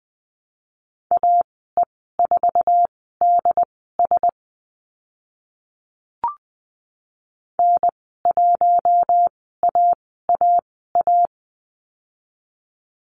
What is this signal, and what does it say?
Morse code, heard as a keyed tone, about 20 wpm, spaced as for 14 wpm, 700 Hz. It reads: AE4DS N1AAA
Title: 069 Lesson 12g - Letter d - Call Signs 20wpm